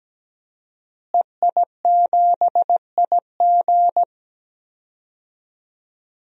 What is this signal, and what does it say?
Morse code, heard as a keyed tone, about 17 wpm, 700 Hz. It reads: EI7IG
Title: DXCC-100 Call Signs - 5 Characters - Repeated 2x - 17wpm